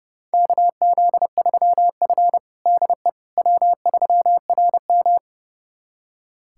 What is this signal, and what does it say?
Morse code, heard as a keyed tone, about 30 wpm, 700 Hz. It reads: KZ3F DE W3RM
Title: USA Callsign DE Callsign - 4 Characters - 30wpm